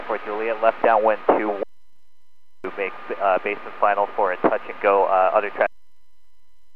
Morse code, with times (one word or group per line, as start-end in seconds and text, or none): none